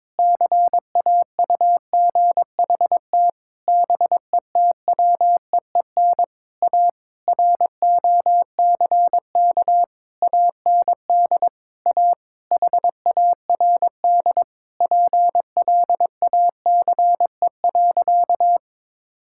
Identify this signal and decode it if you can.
Morse code, heard as a keyed tone, about 22 wpm, 700 Hz. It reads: CAUGHT BETWEEN A ROCK AND A HARD PLACE.
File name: practice / 22wpm / Sentences with 8-Word Phrases - 22wpm